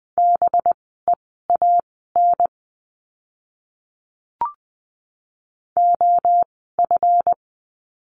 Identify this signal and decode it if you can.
Morse code, heard as a keyed tone, about 20 wpm, spaced as for 14 wpm, 700 Hz. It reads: BEAN OF
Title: 185 Lesson 30f - Letter b - Words 20wpm